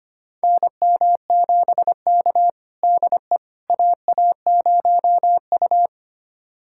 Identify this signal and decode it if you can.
Morse code, heard as a keyed tone, about 25 wpm, 700 Hz. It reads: NM7K DE AA0U
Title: USA Callsign DE Callsign - 4 Characters - 25wpm